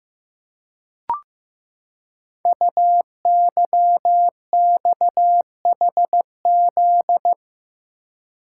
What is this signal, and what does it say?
Morse code, heard as a keyed tone, about 15 wpm, 700 Hz. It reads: UYXHZ
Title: Code Groups - 5 Letters - 15wpm